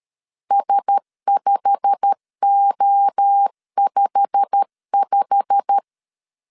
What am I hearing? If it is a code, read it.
Morse code, heard as a keyed tone, about 13 wpm, 800 Hz. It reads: S5O55
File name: LEZIONI / 8 YJ45 / 8 YJ45 12wpm 2151